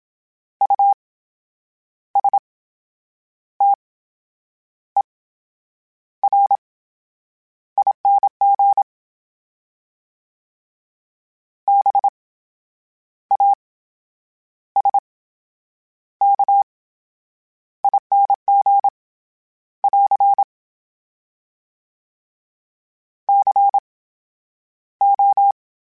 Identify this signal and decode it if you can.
Morse code, heard as a keyed tone, about 27 wpm, 800 Hz. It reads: U S T E R I N G B A S K I N G <AR> C O